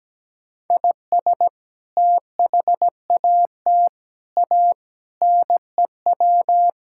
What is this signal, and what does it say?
Morse code, heard as a keyed tone, about 17 wpm, 700 Hz. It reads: IS THAT A NEW
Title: Sentences from Top 1600 Words 17wpm